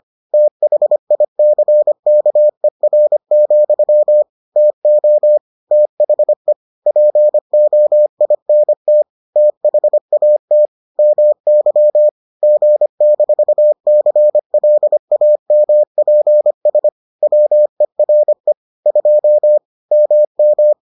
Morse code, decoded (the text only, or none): THICKER, TO THE POINT THAT MY G-CLAMPS WERE 2 MM